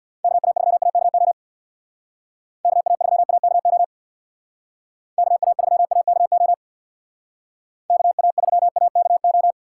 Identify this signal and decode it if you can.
Morse code, heard as a keyed tone, about 62 wpm, 700 Hz. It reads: KA3AKO KA3AKO KA3AKO KA3AKO
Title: Speed Racing - USA 2x3 Call Signs 50wpm